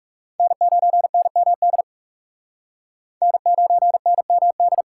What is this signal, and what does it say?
Morse code, heard as a keyed tone, about 37 wpm, 700 Hz. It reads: N9NMD N9NMD
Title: Speed Racing - USA 1x3 Call Signs 30wpm